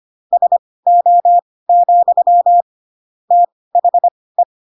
Morse code, heard as a keyed tone, about 25 wpm, spaced as for 18 wpm, 700 Hz. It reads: SO, THE
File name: Pinocchio / Pinocchio ch1 12 25 ik1hge